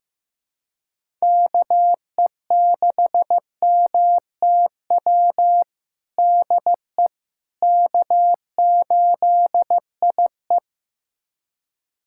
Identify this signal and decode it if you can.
Morse code, heard as a keyed tone, about 15 wpm, 700 Hz. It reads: KE6MTW DE K8IE
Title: USA Callsign DE Callsign 15wpm